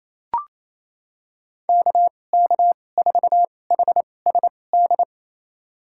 Morse code, heard as a keyed tone, about 28 wpm, spaced as for 20 wpm, 700 Hz. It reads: KK4HSD
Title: USA Call Signs - 6 Characters - 2x Character Spacing - 28wpm